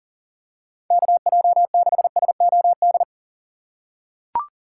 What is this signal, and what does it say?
Morse code, heard as a keyed tone, about 40 wpm, 700 Hz. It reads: KJ6SOD
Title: USA Call Signs - 6 Characters - 3x Character Spacing - 40wpm